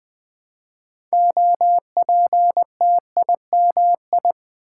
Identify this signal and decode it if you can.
Morse code, heard as a keyed tone, about 20 wpm, 700 Hz. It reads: OPTIMI
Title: Build A Word - 2 Letters At A Time - 8-Letter Words - 20wpm